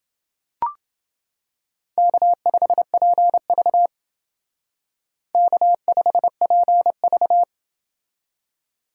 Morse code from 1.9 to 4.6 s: K5PV

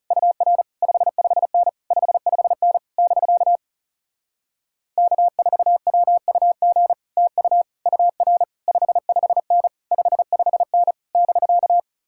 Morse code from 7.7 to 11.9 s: UR 55N 55N <BK>

